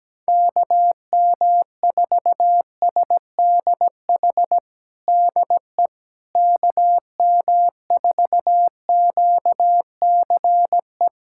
Morse code, heard as a keyed tone, about 17 wpm, 700 Hz. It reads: KM4SDH DE KM4QCE
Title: USA Callsign DE Callsign 17wpm